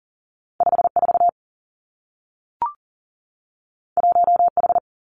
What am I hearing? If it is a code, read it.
Morse code, heard as a keyed tone, about 38 wpm, 700 Hz. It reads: 54 1H
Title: Two Challenging Letter-Number Combinations 40wpm